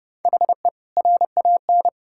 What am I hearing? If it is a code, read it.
Morse code, heard as a keyed tone, about 30 wpm, 700 Hz. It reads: HE RAN
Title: Sentences from Top 1000 Words - Review Part 6 - 30wpm